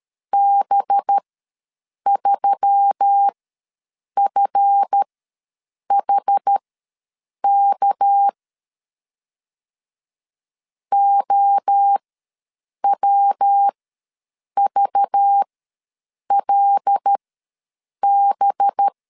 Morse code, heard as a keyed tone, about 13 wpm, spaced as for 7 wpm, 800 Hz. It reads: B3FHK OWVLB
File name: LEZIONI / 7 GF9Z / 7 GF ws 12wpm 3140